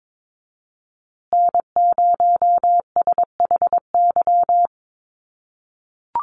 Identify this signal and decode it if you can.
Morse code, heard as a keyed tone, about 22 wpm, 700 Hz. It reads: N0SHY
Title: USA Call Signs Repeated 2x 22wpm